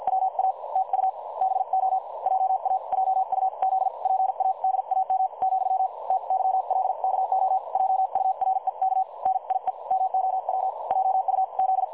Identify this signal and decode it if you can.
Morse code, heard as a keyed tone, about 51 wpm, spaced as for 44 wpm, 760 Hz. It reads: WA TU BO JA JUZ PADAM , TYLKO JUREK NIE MOW JUZ
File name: dlazaawansowanych / nagraniazpasma / websdr recording 3565.0kHz